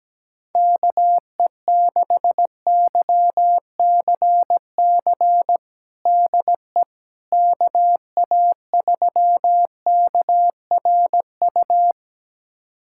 Morse code, heard as a keyed tone, about 17 wpm, 700 Hz. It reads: KE6YCC DE KA3KRU
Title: USA Callsign DE Callsign 17wpm